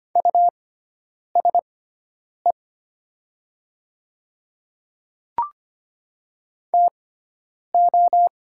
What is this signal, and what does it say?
Morse code, heard as a keyed tone, about 25 wpm, spaced as for 9 wpm, 700 Hz. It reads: USE TO
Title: 6x Character Spacing - Top 100 Words - 25wpm